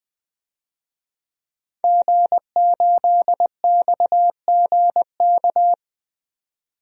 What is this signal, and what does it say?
Morse code, heard as a keyed tone, about 20 wpm, 700 Hz. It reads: G8XGK
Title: DXCC-100 Call Signs - 5 Characters - 20wpm